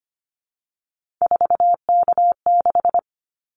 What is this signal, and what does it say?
Morse code, heard as a keyed tone, about 25 wpm, 700 Hz. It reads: 4K6
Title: Callsign Prefixes for Top 100 DXCC - 25wpm